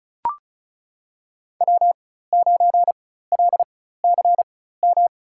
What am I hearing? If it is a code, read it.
Morse code, heard as a keyed tone, about 36 wpm, spaced as for 16 wpm, 700 Hz. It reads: W9LCM
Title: USA Call Signs - 4x Character Spacing - 35wpm